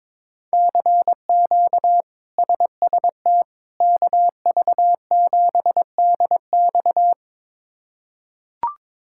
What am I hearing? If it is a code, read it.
Morse code, heard as a keyed tone, about 22 wpm, 700 Hz. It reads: CQ SST KV7DX
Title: K1USN SST QSO Practice - 22wpm